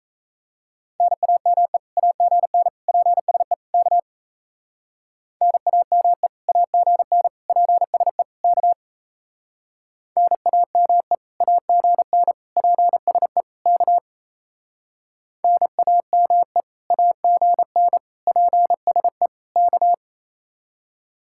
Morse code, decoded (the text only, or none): NAME AGN PSE K NAME AGN PSE K NAME AGN PSE K NAME AGN PSE K